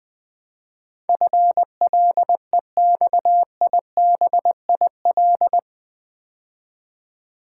Morse code, heard as a keyed tone, about 20 wpm, 700 Hz. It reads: FLEXIBIL
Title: Build A Word - 4 Letters At A Time - 11-Letter Words - 20wpm